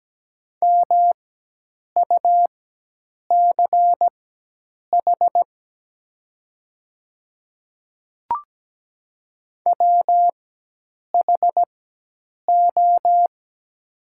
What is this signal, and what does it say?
Morse code, heard as a keyed tone, about 17 wpm, spaced as for 8 wpm, 700 Hz. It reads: MUCH WHO
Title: 4x Character Spacing - Top 300 Words - 17wpm